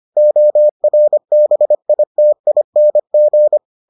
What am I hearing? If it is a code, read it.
Morse code, heard as a keyed tone, about 25 wpm, 600 Hz. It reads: ORBITING